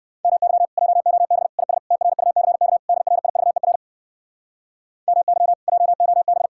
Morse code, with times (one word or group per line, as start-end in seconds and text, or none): 0.2-0.7 MY
0.8-1.5 JOB
1.6-1.8 IS
1.9-2.8 TRUCK
2.9-3.8 DRIVER
5.1-5.5 MY
5.7-6.5 JOB